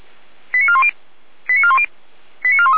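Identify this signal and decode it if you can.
Five-tone selective call (ZVEI), heard as quick runs of five tones, tones of about 70 ms. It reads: 89410 89410 89413
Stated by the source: ZVEI five-tone selective call, code 88801, received by SDR.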